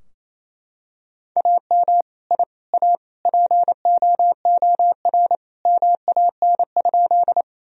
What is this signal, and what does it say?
Morse code, heard as a keyed tone, about 28 wpm, 700 Hz. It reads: AM I A POOR MAN?
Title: Sentences from Top 1000 Words - Review Part 2 - 28wpm